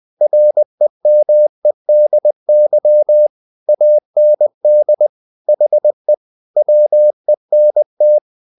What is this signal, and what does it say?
Morse code, heard as a keyed tone, about 20 wpm, 600 Hz. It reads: REMEDY AND HE WENT
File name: the old man and the sea part 2 20wpm